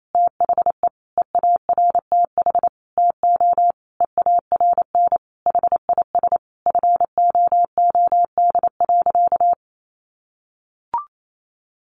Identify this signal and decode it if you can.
Morse code, heard as a keyed tone, about 28 wpm, 700 Hz. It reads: THE EARTH TO EARN HIS FOOD.
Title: Sentences from Top 1500 Words 28wpm